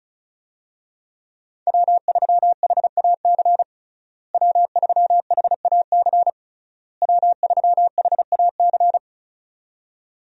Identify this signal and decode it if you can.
Morse code, heard as a keyed tone, about 36 wpm, 700 Hz. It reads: W3HAC W3HAC W3HAC
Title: USA Call Signs - 5 Characters - Repeated 3x - 35wpm